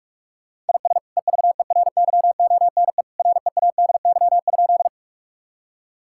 Morse code, heard as a keyed tone, about 46 wpm, 700 Hz. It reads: IS EVERYONE READY?
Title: Sentences from Top 1000 Words - Review Part 5 - 45wpm